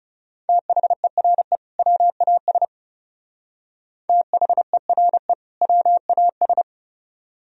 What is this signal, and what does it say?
Morse code, heard as a keyed tone, about 37 wpm, 700 Hz. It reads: THERE WAS THERE WAS
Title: Speed-Racing - Sets of 2 Words 30wpm